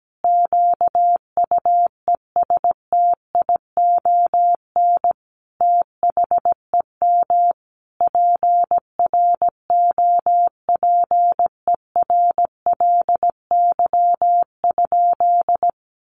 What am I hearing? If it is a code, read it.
Morse code, heard as a keyed tone, about 17 wpm, 700 Hz. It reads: QUESTION THEM PROPERLY?
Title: Sentences from Top 1100 Words 17wpm